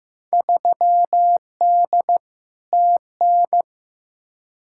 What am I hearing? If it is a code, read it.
Morse code, heard as a keyed tone, about 15 wpm, 700 Hz. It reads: 3D TN
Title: Field Day Exchanges Encore - 15wpm